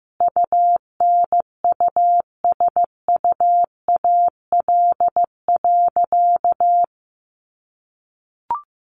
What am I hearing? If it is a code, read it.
Morse code, heard as a keyed tone, about 15 wpm, 700 Hz. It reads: UNUSUAL.
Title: Sentences from Top 1400 Words 15wpm